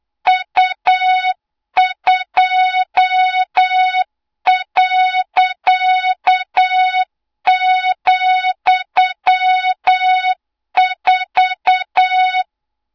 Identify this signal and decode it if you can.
Morse code, heard as a keyed tone, about 8 wpm, 3700 Hz, with noise. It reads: U2.,4
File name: Morze-HA2MN / hang / 20 GYAKORLAT